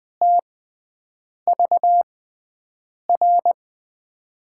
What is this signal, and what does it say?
Morse code, heard as a keyed tone, about 20 wpm, spaced as for 7 wpm, 700 Hz. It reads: TVR
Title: USA Call Signs - 5 Characters - 6x Character Spacing - 20wpm